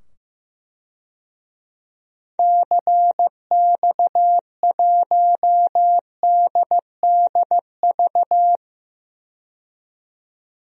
Morse code, heard as a keyed tone, about 15 wpm, 700 Hz. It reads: CX1DDV
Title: DXCC-100 Call Signs - 6 Characters - 15wpm